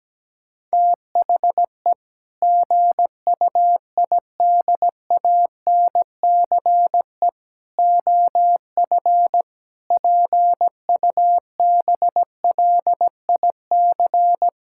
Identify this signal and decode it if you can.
Morse code, heard as a keyed tone, about 17 wpm, 700 Hz. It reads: THE GUIDANCE OF PUBLIC
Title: Sentences from Top 1600 Words 17wpm